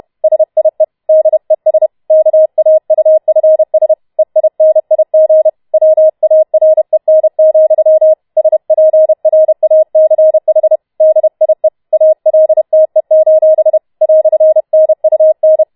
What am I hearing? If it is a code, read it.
Morse code, heard as a keyed tone, about 31 wpm, 630 Hz. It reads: SIE DES KAUFS EINIG WAREN, SPRACH DIE ALTE: "NUN
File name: Schneewittchen Teil4 32wpm